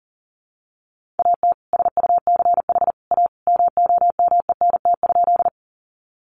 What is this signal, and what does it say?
Morse code, heard as a keyed tone, about 37 wpm, 700 Hz. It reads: AT SUCH A MOMENT?
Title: Sentences from Top 1000 Words 40wpm